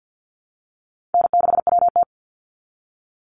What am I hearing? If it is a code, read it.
Morse code, heard as a keyed tone, about 50 wpm, 700 Hz. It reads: N6WT
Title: NA SOTA Chaser Callsigns 50wpm